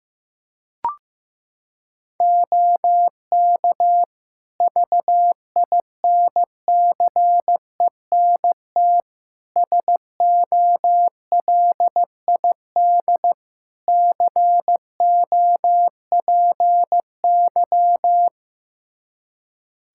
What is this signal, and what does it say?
Morse code, heard as a keyed tone, about 16 wpm, 700 Hz. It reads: OK VINCENT SOLID COPY
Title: Practice QSO - Exchange 3 - More Info - 15wpm